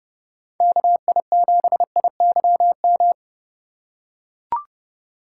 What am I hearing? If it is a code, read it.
Morse code, heard as a keyed tone, about 28 wpm, 700 Hz. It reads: KI7IYM